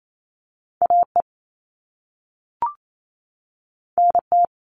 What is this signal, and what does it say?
Morse code, heard as a keyed tone, about 28 wpm, 700 Hz. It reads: AE NT
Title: All Cut Numbers - 2-Digit 28wpm